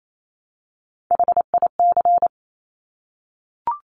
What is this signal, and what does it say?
Morse code, heard as a keyed tone, about 28 wpm, 700 Hz. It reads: HIC
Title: Top Three Letter Combinations in Words 28wpm